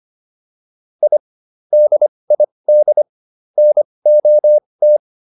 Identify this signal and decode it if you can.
Morse code, heard as a keyed tone, about 25 wpm, spaced as for 20 wpm, 600 Hz. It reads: I DID NOT